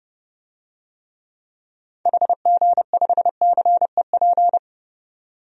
Morse code, heard as a keyed tone, about 30 wpm, 700 Hz. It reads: HG5CEP
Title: DXCC-100 Call Signs - 6 Characters - 30wpm